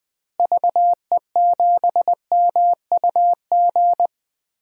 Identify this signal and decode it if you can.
Morse code, heard as a keyed tone, about 20 wpm, 700 Hz. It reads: VE7MUG
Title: DXCC-100 Call Signs - 6 Characters - Repeated 3x - 20wpm